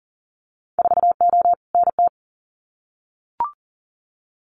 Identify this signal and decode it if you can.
Morse code, heard as a keyed tone, about 40 wpm, 700 Hz. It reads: 4O NT
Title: Winter Field Day Exchanges Encore 40wpm